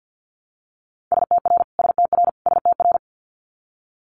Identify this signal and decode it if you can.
Morse code, heard as a keyed tone, about 50 wpm, 690 Hz. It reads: STR STR STR
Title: Multi-letter Phonemes Repeated 3x 50wpm